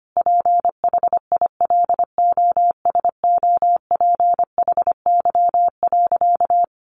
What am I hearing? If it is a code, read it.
Morse code, heard as a keyed tone, about 25 wpm, 700 Hz. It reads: PHILOSOPHY.
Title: Sentences from Top 900 Words 25wpm